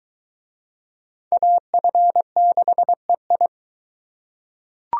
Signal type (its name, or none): Morse code